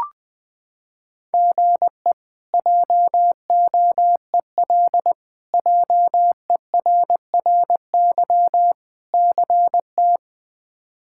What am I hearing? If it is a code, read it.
Morse code, heard as a keyed tone, about 21 wpm, 700 Hz. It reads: GE JOEL JERRY CT
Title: K1USN SST QSO Practice - 20wpm